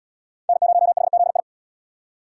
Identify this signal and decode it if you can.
Morse code, heard as a keyed tone, about 72 wpm, 700 Hz. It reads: N0HZI